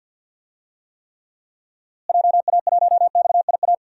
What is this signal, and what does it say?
Morse code, heard as a keyed tone, about 52 wpm, 700 Hz. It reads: JA1XIA
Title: DXCC-100 Call Signs - 6 Characters - Repeated 2x - 50wpm